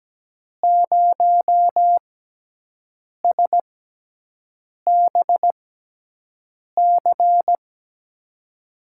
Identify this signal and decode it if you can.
Morse code, heard as a keyed tone, about 17 wpm, spaced as for 6 wpm, 700 Hz. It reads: 0SBC
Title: USA Call Signs - 5 Characters - 6x Character Spacing - 17wpm